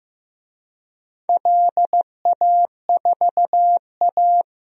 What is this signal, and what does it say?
Morse code, heard as a keyed tone, about 15 wpm, 700 Hz. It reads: LA4A
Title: DXCC-100 Call Signs - 4 Characters - Repeated 3x - 15wpm